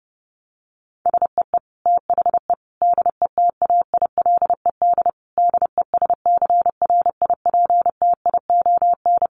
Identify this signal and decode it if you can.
Morse code, heard as a keyed tone, about 30 wpm, 700 Hz. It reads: SEE THE DETAILED DESCRIPTION
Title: Sentences from Top 1500 Words 30wpm